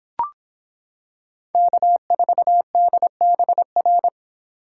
Morse code, heard as a keyed tone, about 26 wpm, 700 Hz. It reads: K4DBR